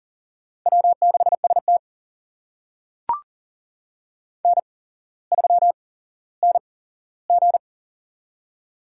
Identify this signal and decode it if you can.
Morse code, heard as a keyed tone, about 42 wpm, spaced as for 17 wpm, 700 Hz. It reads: W6ST N3NG
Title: USA Call Signs - 4 Characters - 8x Character Spacing - 40wpm